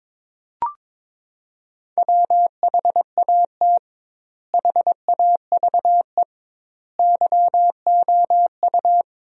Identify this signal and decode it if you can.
Morse code, heard as a keyed tone, about 22 wpm, 700 Hz. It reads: WHAT HAVE YOU
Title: 2x Word-Spacing - Sentences from Top 400 Words - 22wpm